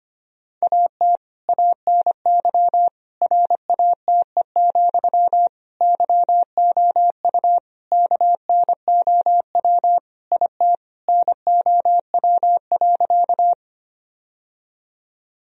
Morse code, heard as a keyed tone, about 25 wpm, 700 Hz. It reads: AT ANY RATE, YOU KNOW IT NOW.
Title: Sentences from Top 1000 Words - Review Part 4 - 25wpm